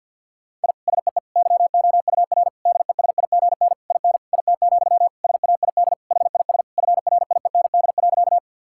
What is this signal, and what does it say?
Morse code, heard as a keyed tone, about 50 wpm, 700 Hz. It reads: I SEE YOUR DESIGN IN IT, SAID HIS FRIEND.
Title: Sentences from Top 1000 Words - Review Part 5 - 50wpm